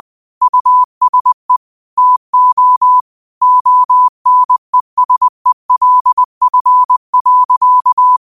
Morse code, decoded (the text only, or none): USE TO ONESELF.